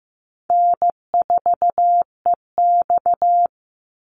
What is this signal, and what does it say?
Morse code, heard as a keyed tone, about 15 wpm, 700 Hz. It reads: N4EX
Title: NA SOTA Activator Callsigns 15wpm